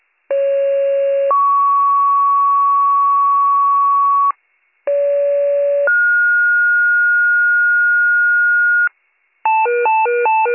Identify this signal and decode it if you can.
Two-tone page, heard as a one-second tone, then a three-second tone, about 550 then 1120 Hz, about 570 then 1430 Hz, with noise. It is heard from 0.3 to 8.9 s.